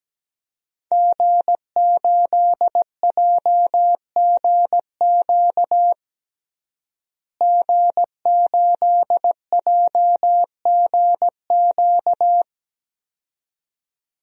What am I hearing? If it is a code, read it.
Morse code, heard as a keyed tone, about 17 wpm, 700 Hz. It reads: G8JGQ G8JGQ
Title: DXCC-100 Call Signs - 5 Characters - Repeated 2x - 17wpm